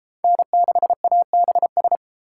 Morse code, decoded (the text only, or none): N6ABS